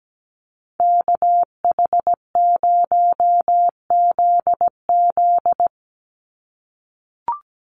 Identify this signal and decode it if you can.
Morse code, heard as a keyed tone, about 17 wpm, 700 Hz. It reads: KH0ZZ
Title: Call Signs - 17wpm